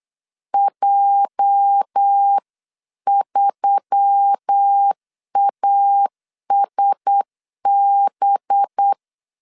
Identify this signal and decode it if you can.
Morse code, heard as a keyed tone, about 8 wpm, 800 Hz. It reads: J3ASB
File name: CW / DET / 1 9 1500C S C 8wpm 5653